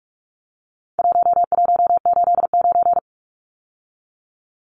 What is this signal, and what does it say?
Morse code, heard as a keyed tone, about 44 wpm, 700 Hz. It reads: 1189
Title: Numbers - 4-Digit 45wpm